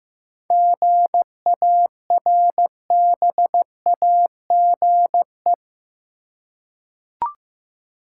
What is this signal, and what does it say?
Morse code, heard as a keyed tone, about 15 wpm, 700 Hz. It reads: GARBAGE
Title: Seven Letter Words - 15wpm